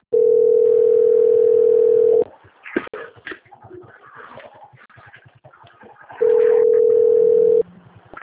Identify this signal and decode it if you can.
Ringback tone (North America), 440 + 480 Hz.